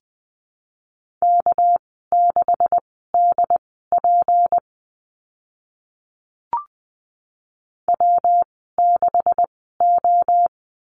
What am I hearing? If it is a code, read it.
Morse code, heard as a keyed tone, about 20 wpm, spaced as for 14 wpm, 700 Hz. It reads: K6DP W6O
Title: 219 Lesson 35g - Number 0 - Call Signs 20wpm